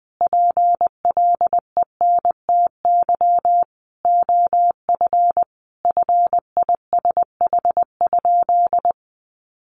Morse code, heard as a keyed tone, about 20 wpm, 700 Hz. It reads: PLENTY OF FISH?
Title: Sentences from Top 1500 Words 20wpm